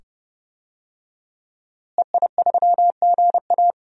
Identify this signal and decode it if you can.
Morse code, heard as a keyed tone, about 30 wpm, 700 Hz. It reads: EI3GA
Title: DXCC-100 Call Signs - 5 Characters - Repeated 2x - 30wpm